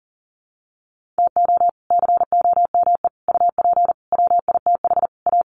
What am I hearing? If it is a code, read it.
Morse code, heard as a keyed tone, about 40 wpm, 700 Hz. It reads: TO COME UP WITH A
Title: Sets of 5 Words 40wpm